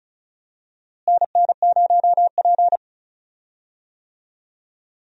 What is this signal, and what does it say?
Morse code, heard as a keyed tone, about 34 wpm, 700 Hz. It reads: NN0P
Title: USA Call Signs - 4 Characters - Repeated 3x - 35wpm